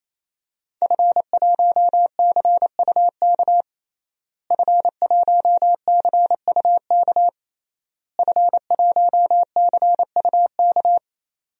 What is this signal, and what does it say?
Morse code, heard as a keyed tone, about 28 wpm, 700 Hz. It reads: F1CUK F1CUK F1CUK
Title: DXCC-100 Call Signs - 5 Characters - Repeated 3x - 28wpm